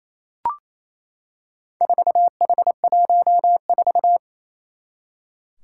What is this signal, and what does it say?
Morse code, heard as a keyed tone, about 28 wpm, 700 Hz. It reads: 4H14
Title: Four Challenging Letter-Number Combinations 28wpm